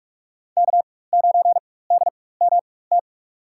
Morse code, heard as a keyed tone, about 46 wpm, spaced as for 21 wpm, 700 Hz. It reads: K9DMT
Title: USA Call Signs - 5 Characters - 4x Character Spacing - 45wpm